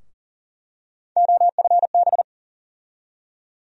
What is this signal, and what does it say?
Morse code, heard as a keyed tone, about 40 wpm, 700 Hz. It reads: OFB
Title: USA Call Sign Postfixes - 3 Letters 40wpm